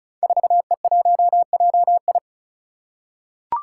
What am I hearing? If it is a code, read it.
Morse code, heard as a keyed tone, about 36 wpm, 700 Hz. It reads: 4E1JI
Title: Five Challenging Letter-Number Combinations 35wpm